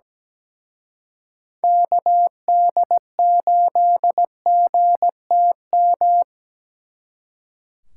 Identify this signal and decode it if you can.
Morse code, heard as a keyed tone, about 20 wpm, 700 Hz. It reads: KD8GTM
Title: Speed Racing - USA 2x3 Call Signs 17wpm